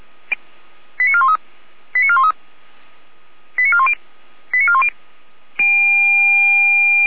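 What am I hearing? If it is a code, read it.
Five-tone selective call (ZVEI), heard as quick runs of five tones, tones of about 70 ms. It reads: 89413 89413 89410 89410
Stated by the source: ZVEI five-tone call received on SDR, recorded by the Orlamünde fire brigade (Feuerwehr Orlamünde).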